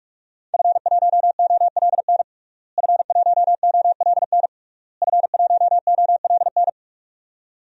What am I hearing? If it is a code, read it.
Morse code, heard as a keyed tone, about 46 wpm, 700 Hz. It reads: F1OLN F1OLN F1OLN
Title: DXCC-100 Call Signs - 5 Characters - Repeated 3x - 45wpm